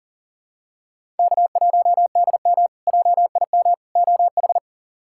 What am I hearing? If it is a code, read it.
Morse code, heard as a keyed tone, about 40 wpm, 700 Hz. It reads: K1DM JIM OH
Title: K1USN SST QSO Practice - 40wpm